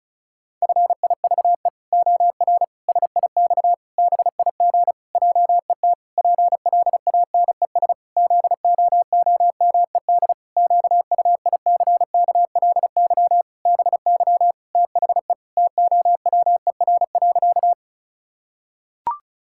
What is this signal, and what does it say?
Morse code, heard as a keyed tone, about 36 wpm, 700 Hz. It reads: FIVE OR SIX BIG JET PLANES ZOOMED QUICKLY BY THE TOWER.
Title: Pangram Challenge v2 - 35wpm